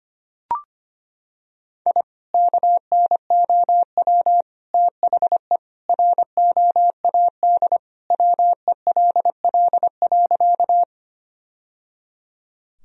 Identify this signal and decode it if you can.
Morse code, heard as a keyed tone, about 25 wpm, 700 Hz. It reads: I KNOW THE ROAD WELL.
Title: Sentences from Top 1000 Words - Review Part 2 - 25wpm